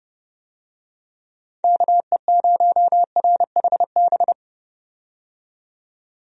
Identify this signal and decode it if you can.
Morse code, heard as a keyed tone, about 32 wpm, 700 Hz. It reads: KE0RHB
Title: USA Call Signs - 6 Characters - 30wpm